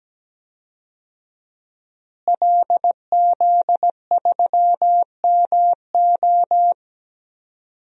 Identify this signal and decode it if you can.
Morse code, heard as a keyed tone, about 17 wpm, 700 Hz. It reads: LZ3MO
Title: DXCC-100 Call Signs - 5 Characters - Repeated 2x - 17wpm